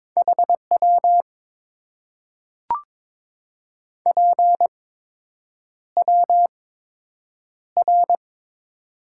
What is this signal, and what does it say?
Morse code, heard as a keyed tone, about 22 wpm, spaced as for 8 wpm, 700 Hz. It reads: HW PWR
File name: Top 90 QSO Elements - 8x Character Spacing - 22wpm